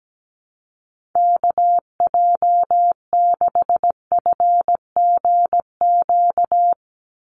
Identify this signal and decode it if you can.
Morse code, heard as a keyed tone, about 17 wpm, 700 Hz. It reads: KJ6FGQ
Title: USA Call Signs 17wpm